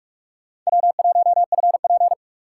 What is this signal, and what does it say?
Morse code, heard as a keyed tone, about 46 wpm, 700 Hz. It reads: W1FP
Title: USA Call Signs - 2x Character Spacing - 45wpm